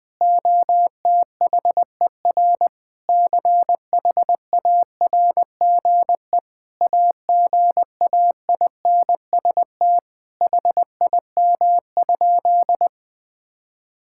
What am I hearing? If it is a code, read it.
Morse code, heard as a keyed tone, about 21 wpm, 700 Hz. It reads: OTHER CHARGE AGAINST HIM?
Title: Sentences from Top 1000 Words - Review Part 6 - 20wpm